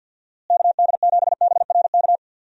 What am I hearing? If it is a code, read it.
Morse code, heard as a keyed tone, about 50 wpm, 700 Hz. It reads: KD7BRK
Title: USA Call Signs - 6 Characters - Repeated 3x - 50wpm